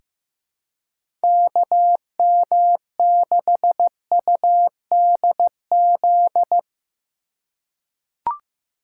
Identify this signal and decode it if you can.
Morse code, heard as a keyed tone, about 15 wpm, 700 Hz. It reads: KM6UDZ
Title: USA Call Signs - 6 Characters - 4x Character Spacing - 15wpm